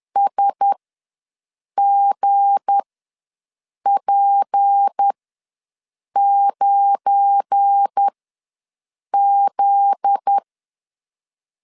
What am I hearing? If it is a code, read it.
Morse code, heard as a keyed tone, about 10 wpm, spaced as for 6 wpm, 800 Hz. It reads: SGP9Z
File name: LEZIONI / 9 28 ws 10wpm 4124